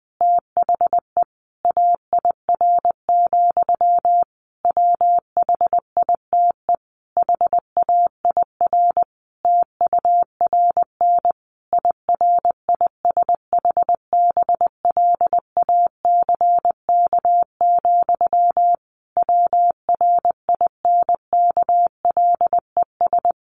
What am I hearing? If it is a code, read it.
Morse code, heard as a keyed tone, about 20 wpm, 700 Hz. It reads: THE AIR, WHITE HAIR TURN IRISH BLACK, WRINKLES
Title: A Sound Of Thunder 20wpm